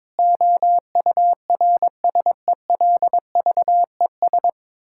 Morse code, heard as a keyed tone, about 22 wpm, 700 Hz. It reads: OURSELVES
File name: Difficult Sentences with Hints - 22wpm